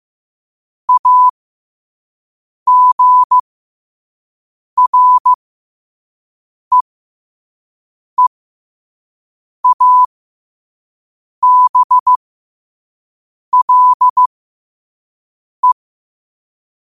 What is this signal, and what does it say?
Morse code, heard as a keyed tone, about 15 wpm, spaced as for 5 wpm, 1000 Hz. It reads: AGREEABLE
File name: qotdfiles / 5wpm